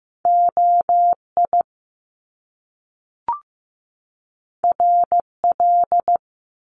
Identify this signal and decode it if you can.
Morse code, heard as a keyed tone, about 15 wpm, 700 Hz. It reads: OI RL
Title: Top Two Letter Combinations in Words 15wpm